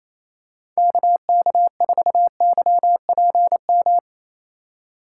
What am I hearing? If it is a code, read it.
Morse code, heard as a keyed tone, about 28 wpm, 700 Hz. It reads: KK4YPM
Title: USA Call Signs - 6 Characters - 4x Character Spacing - 28wpm